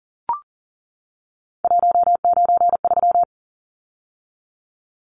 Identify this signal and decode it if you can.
Morse code, heard as a keyed tone, about 40 wpm, 700 Hz. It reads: 193
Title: Numbers - 3-Digit 40wpm